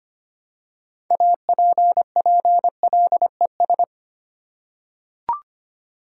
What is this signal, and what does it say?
Morse code, heard as a keyed tone, about 25 wpm, 700 Hz. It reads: APPLES
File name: practice / 25wpm / Easy Words with Hints - 25wpm